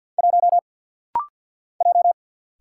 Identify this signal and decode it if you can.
Morse code, heard as a keyed tone, about 50 wpm, 690 Hz. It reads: 1 J
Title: Single Letter-Number - Rapid-Fire 50wpm